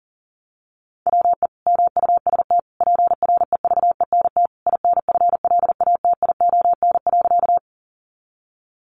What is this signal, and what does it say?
Morse code, heard as a keyed tone, about 40 wpm, 700 Hz. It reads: WE MUST PREVENT INFLATION.
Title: Sentences from Top 1100 Words 40wpm